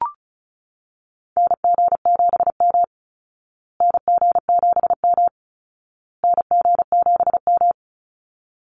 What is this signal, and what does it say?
Morse code, heard as a keyed tone, about 34 wpm, 700 Hz. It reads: NG7M NG7M NG7M
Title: Call Signs Repeated 2x - 35wpm